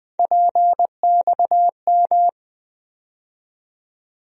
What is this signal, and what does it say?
Morse code, heard as a keyed tone, about 20 wpm, 700 Hz. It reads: PXM
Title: USA Call Sign Postfixes - 3 Letters 20wpm